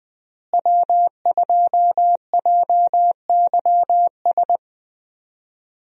Morse code, heard as a keyed tone, about 20 wpm, 700 Hz. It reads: W2JYS
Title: USA Call Signs - 5 Characters - Repeated 2x - 20wpm